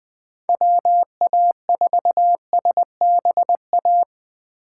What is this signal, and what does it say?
Morse code, heard as a keyed tone, about 21 wpm, 700 Hz. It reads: WA4SBA